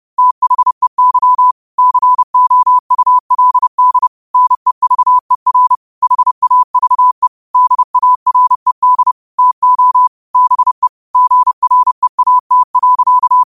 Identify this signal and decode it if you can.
Morse code, heard as a keyed tone, about 30 wpm, 1000 Hz. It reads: THEY COULD NEVER HAVE DARED TO BE GREAT.
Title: 30 WPM morse code quotes for Fri, 01 Mar 2024